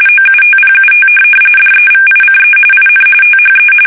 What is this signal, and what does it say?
RTTY, about 45.45 baud. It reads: OUTPUT DEVICE IN THE
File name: fsk850 / teletype-corp